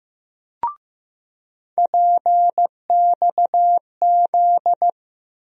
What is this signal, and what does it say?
Morse code, heard as a keyed tone, about 15 wpm, 700 Hz. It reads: PXZ
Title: USA Call Sign Postfixes - 3 Letters 15wpm